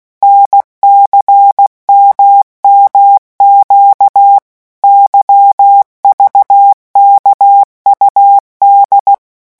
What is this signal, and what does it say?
Morse code, heard as a keyed tone, about 16 wpm, 800 Hz. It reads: NCMMQ YVKUD
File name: morse mp3 / revision3-vitesse 15 mots